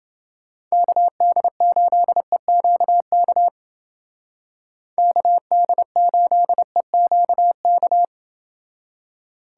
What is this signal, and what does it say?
Morse code, heard as a keyed tone, about 29 wpm, 700 Hz. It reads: KD8EQK KD8EQK